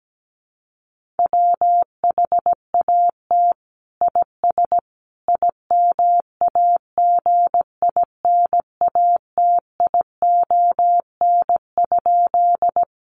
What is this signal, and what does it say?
Morse code, heard as a keyed tone, about 17 wpm, 700 Hz. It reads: WHAT IS IMAGINATION?
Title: Sentences from Top 1300 Words 17wpm